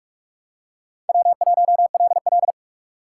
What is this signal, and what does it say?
Morse code, heard as a keyed tone, about 46 wpm, 700 Hz. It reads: W1LL
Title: USA Call Signs - 4 Characters - 2x Character Spacing - 45wpm